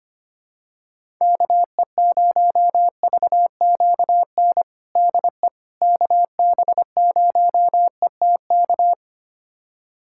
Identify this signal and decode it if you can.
Morse code, heard as a keyed tone, about 25 wpm, 700 Hz. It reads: KE0VQN DE KB0ETK
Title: USA Callsign DE Callsign 25wpm